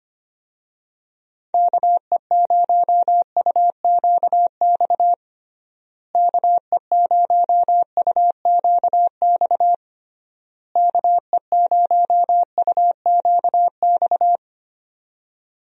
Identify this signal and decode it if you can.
Morse code, heard as a keyed tone, about 25 wpm, 700 Hz. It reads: KE0UQX KE0UQX KE0UQX
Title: USA Call Signs - 6 Characters - Repeated 3x - 25wpm